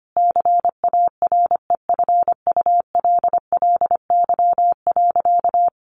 Morse code, heard as a keyed tone, about 25 wpm, 700 Hz. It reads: CAREFULLY.
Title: Sentences from Top 1300 Words 25wpm